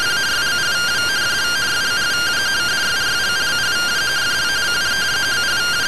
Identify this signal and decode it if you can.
Two-tone teleprinter signal, about 170 Hz shift.
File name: findings / MR SAGGITAL fm10 12mm 6mm.gen-17 rtty rx